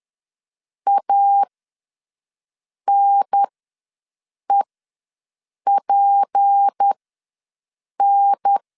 Morse code, heard as a keyed tone, about 10 wpm, spaced as for 6 wpm, 800 Hz. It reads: ANEPN
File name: LEZIONI / 8 YJ45 / 8 YJ ws 10wpm 4043